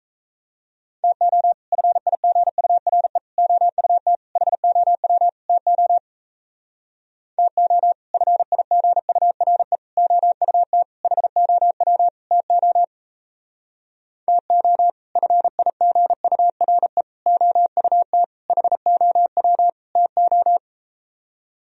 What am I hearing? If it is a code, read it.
Morse code, heard as a keyed tone, about 38 wpm, 700 Hz. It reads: TO FIGURE OUT HOW TO TO FIGURE OUT HOW TO TO FIGURE OUT HOW TO